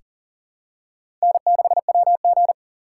Morse code, heard as a keyed tone, about 40 wpm, 700 Hz. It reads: N6WG
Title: USA Call Signs - 4 Characters - 6x Character Spacing - 40wpm